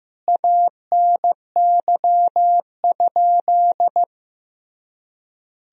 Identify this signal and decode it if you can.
Morse code, heard as a keyed tone, about 15 wpm, 700 Hz. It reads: ANY?
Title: Sentences from Top 300 Words and Abbreviations 15wpm